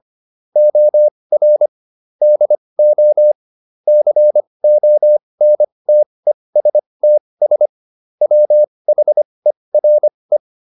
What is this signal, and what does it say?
Morse code, heard as a keyed tone, about 25 wpm, spaced as for 20 wpm, 600 Hz. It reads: OR DO CONTESTS WHERE
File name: SS206 20